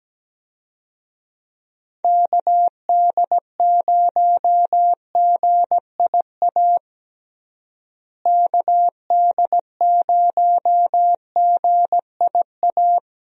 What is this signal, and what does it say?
Morse code, heard as a keyed tone, about 17 wpm, 700 Hz. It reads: KD0GIA KD0GIA
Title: USA Call Signs - 6 Characters - Repeated 3x - 17wpm